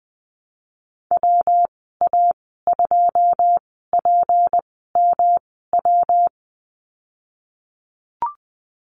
Morse code, heard as a keyed tone, about 19 wpm, spaced as for 14 wpm, 700 Hz. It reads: WA2PMW
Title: 173 Lesson 28g - Number 9 - Call Signs 20wpm